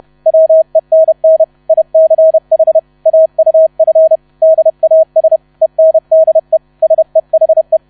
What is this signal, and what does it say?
Morse code, heard as a keyed tone, about 31 wpm, 640 Hz. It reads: WENN ICH AUF DAS ENDE SEHE
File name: 30wpm